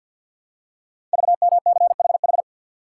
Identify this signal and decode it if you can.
Morse code, heard as a keyed tone, about 50 wpm, 700 Hz. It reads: VMCHH
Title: Code Groups - 5 Letter:Numbers - 50wpm